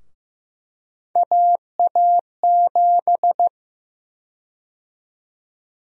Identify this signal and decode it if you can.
Morse code, heard as a keyed tone, about 15 wpm, 700 Hz. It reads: AA7
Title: USA Call Sign Prefixes - 15wpm